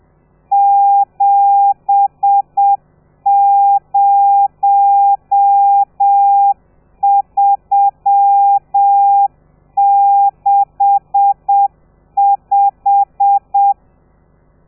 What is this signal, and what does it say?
Morse code, heard as a keyed tone, about 7 wpm, 800 Hz. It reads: 70365
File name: telegrafia / i2tfj / Lez13 30